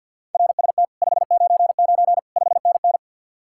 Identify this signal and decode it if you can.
Morse code, heard as a keyed tone, about 64 wpm, 700 Hz. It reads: RST 599 5NN